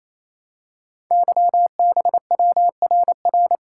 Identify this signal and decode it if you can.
Morse code, heard as a keyed tone, about 28 wpm, 700 Hz. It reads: YBWRR
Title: Code Groups - 5 Letters - 28wpm